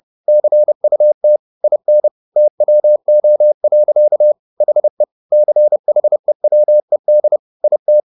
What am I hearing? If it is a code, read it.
Morse code, heard as a keyed tone, about 30 wpm, 600 Hz. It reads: CUT IN TWO. HE CHEWED IT
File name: the old man and the sea / the old man and the sea 1 30wpm